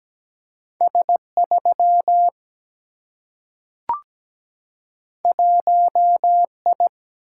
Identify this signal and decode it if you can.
Morse code, heard as a keyed tone, about 17 wpm, 700 Hz. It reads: S3 1I
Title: Single-Letter Single-Number 17wpm